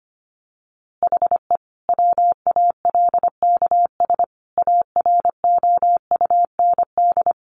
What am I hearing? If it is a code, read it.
Morse code, heard as a keyed tone, about 25 wpm, 700 Hz. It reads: HE WALKS AROUND
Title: Easy Phrases 25wpm